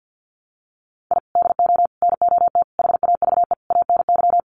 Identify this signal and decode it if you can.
Morse code, heard as a keyed tone, about 48 wpm, 700 Hz. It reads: I DO NOT HAVE ANY
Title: Sets of 5 Words 50wpm